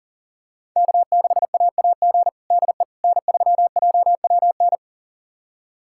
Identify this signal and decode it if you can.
Morse code, heard as a keyed tone, about 37 wpm, 700 Hz. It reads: K6AAG DE N3JWN
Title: USA Callsign DE Callsign - 5 Characters - 40wpm